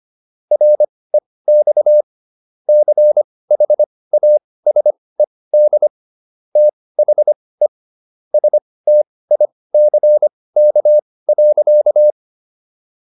Morse code, heard as a keyed tone, about 25 wpm, spaced as for 18 wpm, 600 Hz. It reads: REX CHASED THE STICK.